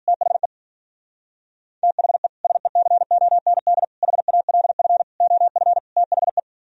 Morse code, heard as a keyed tone, about 51 wpm, 700 Hz. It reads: THE THE SECOND HALF OF THE